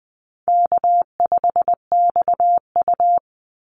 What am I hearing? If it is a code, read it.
Morse code, heard as a keyed tone, about 20 wpm, 700 Hz. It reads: K5XU
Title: Call Signs Repeated 3x - 20wpm